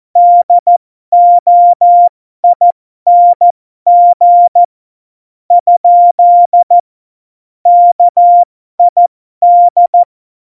Morse code, heard as a keyed tone, about 14 wpm, spaced as for 11 wpm, 700 Hz. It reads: DOING? KID
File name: Morsekurs / Witze